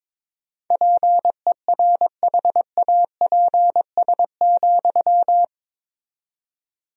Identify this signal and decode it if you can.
Morse code, heard as a keyed tone, about 22 wpm, 700 Hz. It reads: PERHAPS,
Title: 4x Word-Spacing - Sentences from Top 500 Words - 22wpm